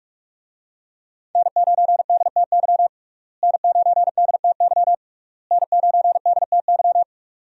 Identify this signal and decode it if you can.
Morse code, heard as a keyed tone, about 46 wpm, 700 Hz. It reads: N9DTY N9DTY N9DTY
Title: USA Call Signs - 5 Characters - Repeated 3x - 45wpm